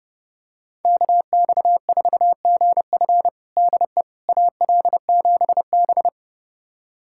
Morse code, heard as a keyed tone, about 30 wpm, 700 Hz. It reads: KX4GF DE AL7B
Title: USA Callsign DE Callsign 30wpm